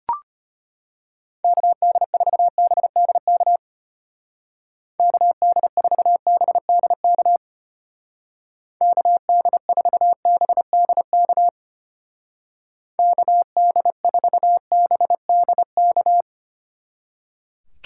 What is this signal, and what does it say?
Morse code, heard as a keyed tone, about 31 wpm, 700 Hz. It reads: KD4BDK KD4BDK KD4BDK KD4BDK
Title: Speed Racing - USA 2x3 Call Signs 25wpm